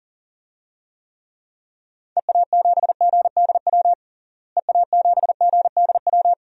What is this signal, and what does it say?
Morse code, heard as a keyed tone, about 37 wpm, 700 Hz. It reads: EA7GDW EA7GDW
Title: DXCC-100 Call Signs - 6 Characters - Repeated 2x - 40wpm